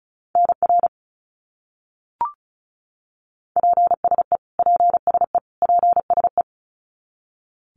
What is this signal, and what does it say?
Morse code, heard as a keyed tone, about 34 wpm, 700 Hz. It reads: NR PSE PSE PSE
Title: Abbreviations Repeated 3x - 35wpm